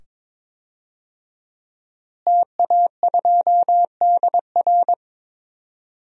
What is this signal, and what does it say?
Morse code, heard as a keyed tone, about 21 wpm, 700 Hz. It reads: TA2DR
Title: DXCC-100 Call Signs - 5 Characters - Repeated 2x - 22wpm